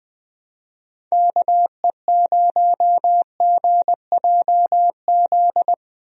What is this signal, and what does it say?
Morse code, heard as a keyed tone, about 21 wpm, 700 Hz. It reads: KE0GJZ
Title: USA Call Signs - 6 Characters - Repeated 2x - 20wpm